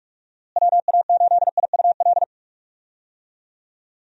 Morse code, heard as a keyed tone, about 41 wpm, 700 Hz. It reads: WA8IUL
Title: USA Call Signs - 6 Characters - Repeated 3x - 45wpm